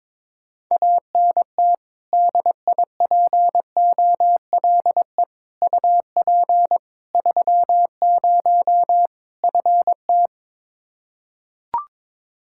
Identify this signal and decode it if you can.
Morse code, heard as a keyed tone, about 22 wpm, 700 Hz. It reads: ANT DIPOLE UP 30 FT
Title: K1USN SST QSO Practice 22wpm